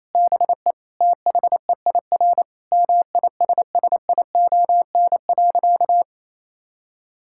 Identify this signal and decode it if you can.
Morse code, heard as a keyed tone, about 28 wpm, 700 Hz. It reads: BE THEIR MISSION.
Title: Sentences from Top 1600 Words 28wpm